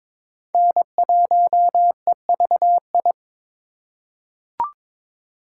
Morse code, heard as a keyed tone, about 22 wpm, 700 Hz. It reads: N1EVI
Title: USA Call Signs - 5 Characters - 2x Character Spacing - 22wpm